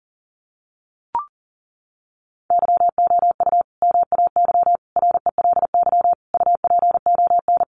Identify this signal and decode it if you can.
Morse code, heard as a keyed tone, about 37 wpm, 700 Hz. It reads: YOU MAY RELY UPON
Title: Sentences from Top 1200 Words 40wpm